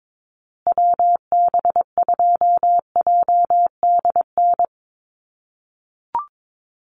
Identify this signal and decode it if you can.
Morse code, heard as a keyed tone, about 22 wpm, 700 Hz. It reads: WB2JDN
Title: USA Call Signs Repeated 2x 22wpm